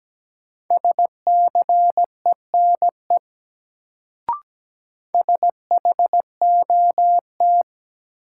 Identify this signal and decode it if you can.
Morse code, heard as a keyed tone, about 17 wpm, 700 Hz. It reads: SCENE SHOT
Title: Sentences from Top 1100 Words - New Word Drill - 17wpm